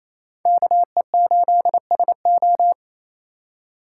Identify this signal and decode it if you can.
Morse code, heard as a keyed tone, about 36 wpm, 700 Hz. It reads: KE8SO